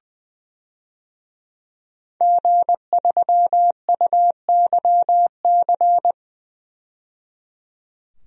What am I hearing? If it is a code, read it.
Morse code, heard as a keyed tone, about 20 wpm, 700 Hz. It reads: G3UYC
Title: DXCC-100 Call Signs - 5 Characters - 20wpm